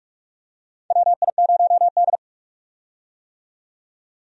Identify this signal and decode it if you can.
Morse code, heard as a keyed tone, about 44 wpm, 700 Hz. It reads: WI0D
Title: USA Call Signs - 4 Characters - 45wpm